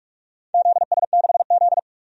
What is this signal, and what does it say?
Morse code, heard as a keyed tone, about 46 wpm, 700 Hz. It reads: ZS6Z
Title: DXCC-100 Call Signs - 4 Characters - Repeated 3x - 45wpm